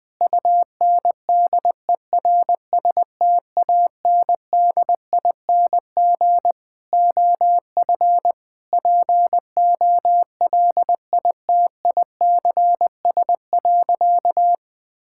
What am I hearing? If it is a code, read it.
Morse code, heard as a keyed tone, about 20 wpm, 700 Hz. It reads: UNDERSTANDING OF POLITICS.
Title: Sentences from Top 1000 Words - Review Part 6 - 20wpm